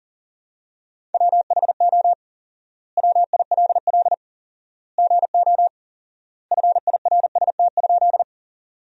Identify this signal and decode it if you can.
Morse code, heard as a keyed tone, about 40 wpm, 700 Hz. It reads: WHO WILL GO FIRST?
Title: 4x Word-Spacing - Sentences from Top 100 Words - 40wpm